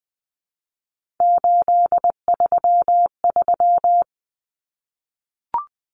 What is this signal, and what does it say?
Morse code, heard as a keyed tone, about 20 wpm, 700 Hz. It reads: 833
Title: Common Cut Numbers - 3-Digit 20wpm